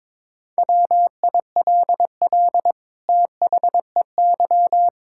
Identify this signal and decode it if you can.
Morse code, heard as a keyed tone, about 22 wpm, 700 Hz. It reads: WILL THEY
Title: Sentences from Top 1600 Words 22wpm